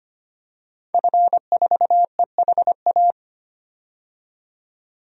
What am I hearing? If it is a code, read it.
Morse code, heard as a keyed tone, about 25 wpm, 700 Hz. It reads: F4EHA
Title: DXCC-100 Call Signs - 5 Characters - 25wpm